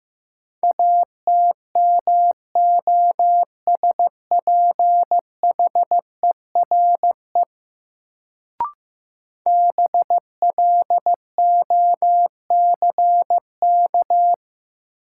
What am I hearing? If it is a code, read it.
Morse code, heard as a keyed tone, about 15 wpm, 700 Hz. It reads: ATMOSPHERE BLOCK
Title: Sentences from Top 1600 Words - New Word Drill - 15wpm